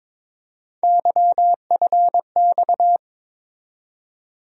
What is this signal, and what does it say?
Morse code, heard as a keyed tone, about 22 wpm, 700 Hz. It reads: YFX
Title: USA Call Sign Postfixes - 3 Letters 22wpm